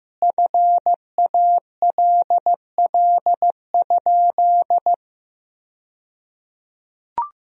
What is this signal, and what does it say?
Morse code, heard as a keyed tone, about 15 wpm, 700 Hz. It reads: FALL?
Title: 2x Word-Spacing - Sentences from Top 500 Words - 15wpm